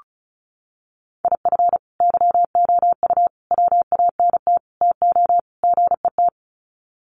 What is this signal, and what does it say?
Morse code, heard as a keyed tone, about 34 wpm, 700 Hz. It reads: IF YOU WANT TO GET
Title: Sets of 5 Words Encore 35wpm